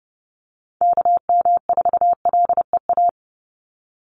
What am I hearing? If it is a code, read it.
Morse code, heard as a keyed tone, about 30 wpm, 700 Hz. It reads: KM4LEA
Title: USA Call Signs 30wpm